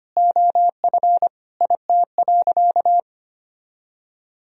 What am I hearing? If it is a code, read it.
Morse code, heard as a keyed tone, about 25 wpm, 700 Hz. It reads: OF IT.
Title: Sentences from Top 500 Words and Abbreviations 25wpm